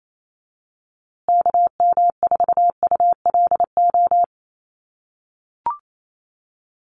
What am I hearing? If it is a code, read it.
Morse code, heard as a keyed tone, about 28 wpm, 700 Hz. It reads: KM4ULO
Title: USA Call Signs Repeated 3x 28wpm